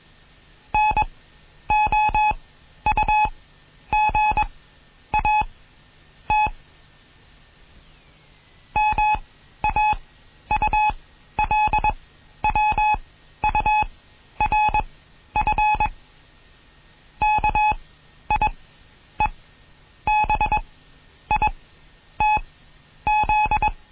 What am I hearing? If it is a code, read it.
Morse code, heard as a keyed tone, about 21 wpm, spaced as for 10 wpm, 850 Hz. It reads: NOUGAT MAULWURF KIEBITZ